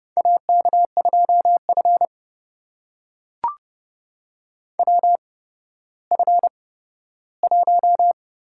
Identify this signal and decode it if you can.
Morse code, heard as a keyed tone, about 31 wpm, spaced as for 15 wpm, 700 Hz. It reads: AK2F WF1
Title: USA Call Signs - 4 Characters - 8x Character Spacing - 30wpm